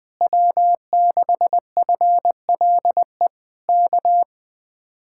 Morse code, heard as a keyed tone, about 19 wpm, 700 Hz. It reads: W6FLE K
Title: Speed-Racing Common QSO Phrases - 20wpm